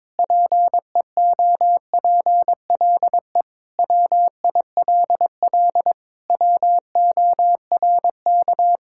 Morse code, heard as a keyed tone, about 22 wpm, 700 Hz. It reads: PEOPLE WILL WORK
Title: Sentences from Top 1000 Words - Review Part 6 - 22wpm